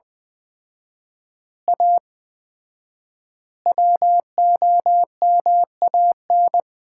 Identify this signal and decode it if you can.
Morse code, heard as a keyed tone, about 20 wpm, 700 Hz. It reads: A WOMAN
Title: 4x Word-Spacing - Sets of 5 Words - 20wpm